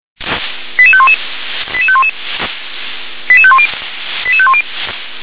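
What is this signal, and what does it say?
Five-tone selective call (ZVEI), heard as quick runs of five tones, tones of about 70 ms. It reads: 88511 88511 88510 88510
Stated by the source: ZVEI five-tone selective call received on an SDR.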